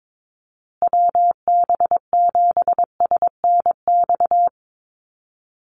Morse code, heard as a keyed tone, about 22 wpm, 700 Hz. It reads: WB7SNX
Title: USA Call Signs Repeated 3x 22wpm